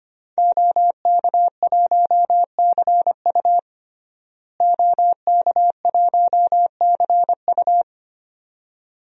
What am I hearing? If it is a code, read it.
Morse code, heard as a keyed tone, about 25 wpm, 700 Hz. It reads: OK1CU OK1CU
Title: DXCC-100 Call Signs - 5 Characters - Repeated 2x - 25wpm